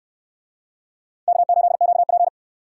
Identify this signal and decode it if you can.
Morse code, heard as a keyed tone, about 63 wpm, 700 Hz. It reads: K8QP